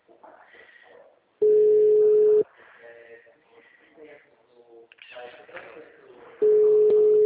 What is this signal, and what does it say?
Ringback tone (Europe, 425 Hz), with speech.